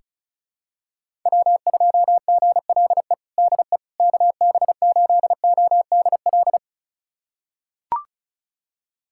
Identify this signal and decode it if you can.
Morse code, heard as a keyed tone, about 36 wpm, 700 Hz. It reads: W2GLE DE KB8ODL
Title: USA Callsign DE Callsign 35wpm